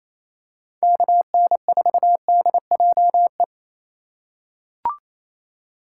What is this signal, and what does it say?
Morse code, heard as a keyed tone, about 27 wpm, 700 Hz. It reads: KN4DJE
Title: USA Call Signs - 6 Characters - 4x Character Spacing - 28wpm